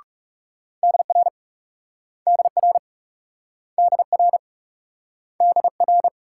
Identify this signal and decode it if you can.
Morse code, heard as a keyed tone, about 38 wpm, 700 Hz. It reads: DR DR DR DR